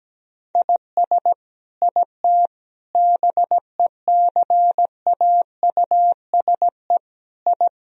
Morse code, heard as a keyed tone, about 17 wpm, 700 Hz. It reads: IS IT BECAUSE I
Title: Sentences from Top 1000 Words - Review Part 3 - 17wpm